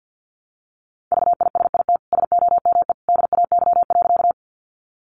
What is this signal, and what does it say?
Morse code, heard as a keyed tone, about 50 wpm, 700 Hz. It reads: VISIT SOME DAY.
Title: Sentences from Top 900 Words 50wpm